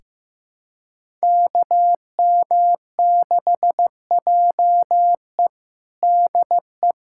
Morse code, heard as a keyed tone, about 15 wpm, 700 Hz. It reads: KM6JE DE
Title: USA Callsign DE Callsign 15wpm